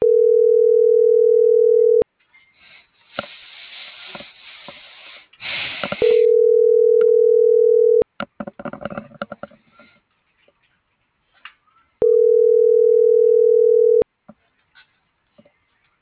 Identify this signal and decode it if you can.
Ringback tone (North America), 440 + 480 Hz.
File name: CENTRAL MAERSK / out sandovalmsk - .69593